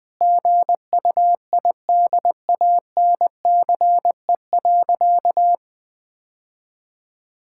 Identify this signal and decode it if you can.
Morse code, heard as a keyed tone, about 20 wpm, 700 Hz. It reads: GUIDANCE.
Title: Sentences from Top 1600 Words 20wpm